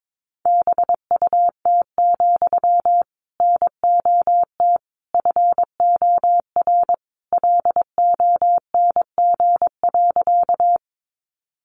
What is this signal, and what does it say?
Morse code, heard as a keyed tone, about 22 wpm, 700 Hz. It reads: BUT, NOT FOR LONG.
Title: Sentences from Top 600 Words 22wpm